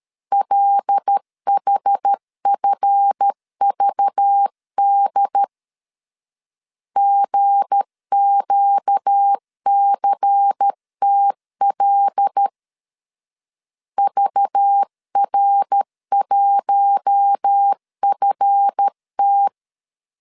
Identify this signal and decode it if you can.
Morse code, heard as a keyed tone, about 13 wpm, 800 Hz. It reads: LHFVD GQCTL VR1FT
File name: LEZIONI / 7 GF9Z / 7 GF 12wpm 2122